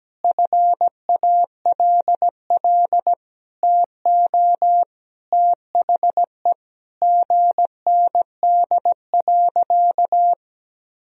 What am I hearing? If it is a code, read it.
Morse code, heard as a keyed tone, about 17 wpm, 700 Hz. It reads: FALL TO THE GND.
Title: Sentences from Top 500 Words and Abbreviations 17wpm